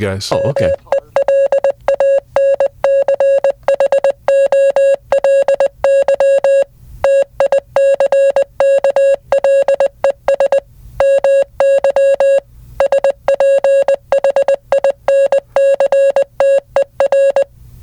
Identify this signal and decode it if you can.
Morse code, heard as a keyed tone, about 20 wpm, 550 Hz, with speech. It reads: MELANCHOLY TICKLES MY SPHINCTER